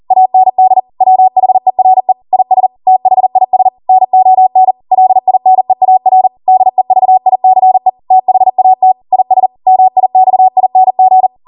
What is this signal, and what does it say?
Morse code, heard as a keyed tone, about 40 wpm, 750 Hz. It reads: AND WHERE IS THIS NON LINEAR DEVICE THAT IS MIXING
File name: Morse / 40WPM